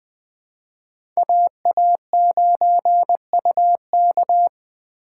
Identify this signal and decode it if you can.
Morse code, heard as a keyed tone, about 20 wpm, 700 Hz. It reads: AA9UK
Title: USA Call Signs - 5 Characters - 4x Character Spacing - 20wpm